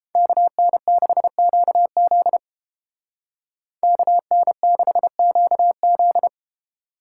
Morse code, heard as a keyed tone, about 30 wpm, 700 Hz. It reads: KN6QZ KN6QZ